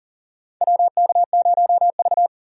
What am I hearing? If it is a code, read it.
Morse code, heard as a keyed tone, about 40 wpm, 700 Hz. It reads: WK0V
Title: USA Call Signs - 4 Characters - 40wpm